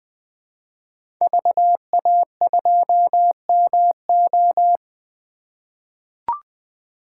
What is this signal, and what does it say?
Morse code, heard as a keyed tone, about 20 wpm, 700 Hz. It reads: VA2MO
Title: 2023 SOTA Chaser Call Signs - 20wpm